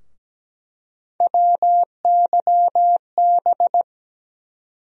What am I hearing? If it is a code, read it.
Morse code, heard as a keyed tone, about 17 wpm, 700 Hz. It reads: WYB
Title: USA Call Sign Postfixes - 3 Letters 17wpm